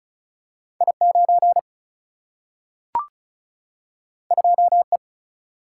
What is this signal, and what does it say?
Morse code, heard as a keyed tone, about 36 wpm, 700 Hz. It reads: I 9 2 E
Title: Single-Letter Single-Number 35wpm